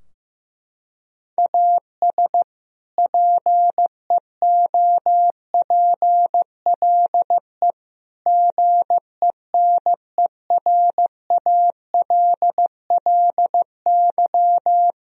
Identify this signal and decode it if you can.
Morse code, heard as a keyed tone, about 15 wpm, 700 Hz. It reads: AS PEOPLE GENERALLY
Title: Sentences from Top 1000 Words - Review Part 4 - 15wpm